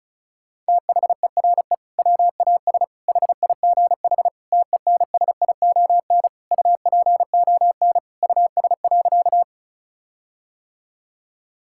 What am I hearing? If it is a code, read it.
Morse code, heard as a keyed tone, about 34 wpm, 700 Hz. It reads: THERE WAS HIGH TENSION UPON US.
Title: Sentences from Top 1600 Words 35wpm